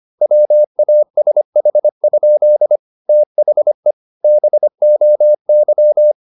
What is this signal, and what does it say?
Morse code, heard as a keyed tone, about 25 wpm, 600 Hz. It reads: WASH? THE BOY
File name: the old man and the sea part 1 25wpm